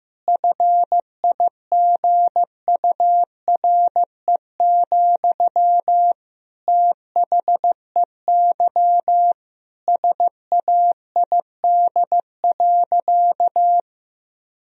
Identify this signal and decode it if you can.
Morse code, heard as a keyed tone, about 16 wpm, 700 Hz. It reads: FIGURE, THEY SAID.
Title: Sentences from Top 1000 Words - Review Part 2 - 15wpm